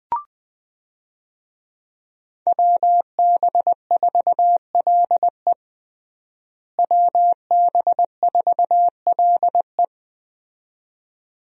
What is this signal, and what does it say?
Morse code, heard as a keyed tone, about 20 wpm, 700 Hz. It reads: WB4LE WB4LE
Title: USA Call Signs - 5 Characters - Repeated 2x - 20wpm